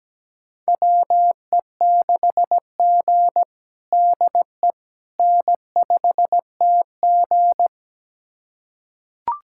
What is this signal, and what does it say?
Morse code, heard as a keyed tone, about 17 wpm, 700 Hz. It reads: WE6G DE N5TG
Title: USA Callsign DE Callsign - 4 Characters - 17wpm